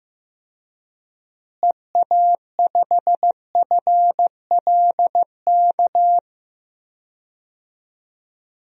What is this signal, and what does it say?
Morse code, heard as a keyed tone, about 15 wpm, 700 Hz. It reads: EA5FLK
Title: DXCC-100 Call Signs - 6 Characters - Repeated 3x - 15wpm